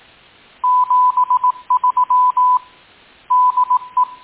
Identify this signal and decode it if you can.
Morse code, heard as a keyed tone, about 18 wpm, 1000 Hz. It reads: 73 DE